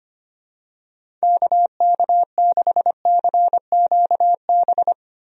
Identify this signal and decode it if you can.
Morse code, heard as a keyed tone, about 25 wpm, 700 Hz. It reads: KK6CQB
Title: USA Call Signs - 6 Characters - Repeated 2x - 25wpm